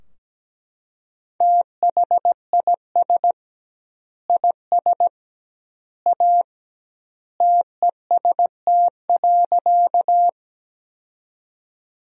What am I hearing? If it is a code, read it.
Morse code, heard as a keyed tone, about 17 wpm, 700 Hz. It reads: THIS IS A TEST.
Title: 2x Word-Spacing - Sentences from Top 500 Words - 17wpm